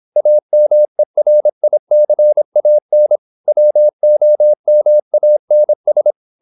Morse code, heard as a keyed tone, about 26 wpm, 600 Hz. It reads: AMERICAN WOMANS
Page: Never 26wpm | CW med Gnister